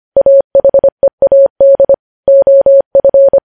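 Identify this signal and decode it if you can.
Morse code, heard as a keyed tone, about 25 wpm, 550 Hz. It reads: AHEAD OF
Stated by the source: Morse code practice of news headlines at 25 wpm, opening with vvv.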